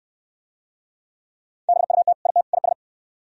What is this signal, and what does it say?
Morse code, heard as a keyed tone, about 66 wpm, 700 Hz. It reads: BUT IT IS